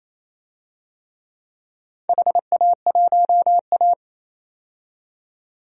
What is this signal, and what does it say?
Morse code, heard as a keyed tone, about 28 wpm, 700 Hz. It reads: HA1A
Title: DXCC-100 Call Signs - 4 Characters - 28wpm